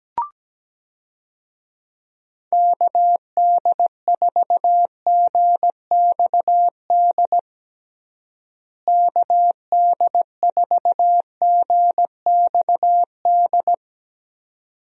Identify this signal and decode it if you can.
Morse code, heard as a keyed tone, about 17 wpm, 700 Hz. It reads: KD4GXD KD4GXD
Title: USA Call Signs - 6 Characters - Repeated 2x - 17wpm